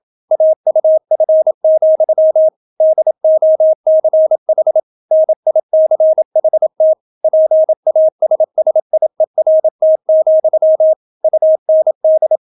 Morse code, heard as a keyed tone, about 28 wpm, 630 Hz. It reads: AUF, DOCH NICHT PASSIERT, UND